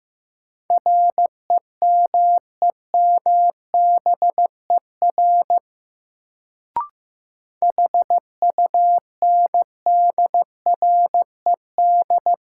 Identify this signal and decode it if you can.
Morse code, heard as a keyed tone, about 15 wpm, 700 Hz. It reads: REMEMBER HUNDRED
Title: Sentences from Top 400 Words - New Word Drill - 15wpm